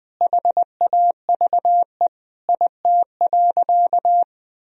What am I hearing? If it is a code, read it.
Morse code, heard as a keyed tone, about 21 wpm, 700 Hz. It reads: HAVE IT.
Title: Sentences from Top 1000 Words - Review Part 3 - 20wpm